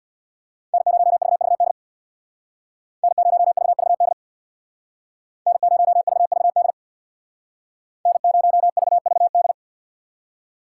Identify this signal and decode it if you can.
Morse code, heard as a keyed tone, about 65 wpm, 700 Hz. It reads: N0VVD N0VVD N0VVD N0VVD